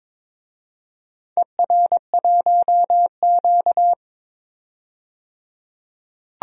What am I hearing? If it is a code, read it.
Morse code, heard as a keyed tone, about 22 wpm, 700 Hz. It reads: ER1Q
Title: DXCC-100 Call Signs - 4 Characters - 22wpm